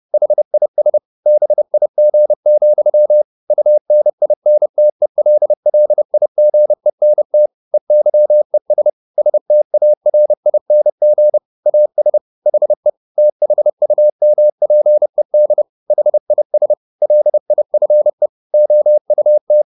Morse code, heard as a keyed tone, about 30 wpm, 600 Hz. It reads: HIS BIG, UNINTELLIGENT EYES STARING AS HE THUMPED HIS LIFE OUT
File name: the old man and the sea / the old man and the sea 1 30wpm